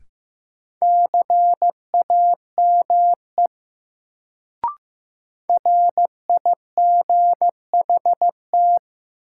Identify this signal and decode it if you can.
Morse code, heard as a keyed tone, about 15 wpm, 700 Hz. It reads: CAME RIGHT
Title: Sentences from Top 200 Words - New Word Drill - 15wpm